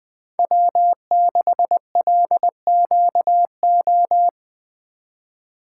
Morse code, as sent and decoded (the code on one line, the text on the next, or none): .-- -.... .-.. --.- ---
W6LQO